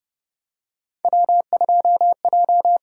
Morse code, heard as a keyed tone, about 30 wpm, 700 Hz. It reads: W2J
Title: USA Call Signs - 5 Characters - 6x Character Spacing - 30wpm